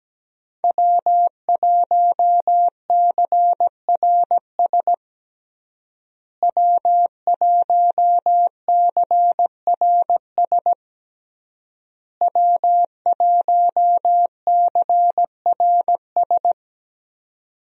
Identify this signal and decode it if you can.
Morse code, heard as a keyed tone, about 17 wpm, 700 Hz. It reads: W1CRS W1CRS W1CRS
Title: USA Call Signs - 5 Characters - Repeated 3x - 17wpm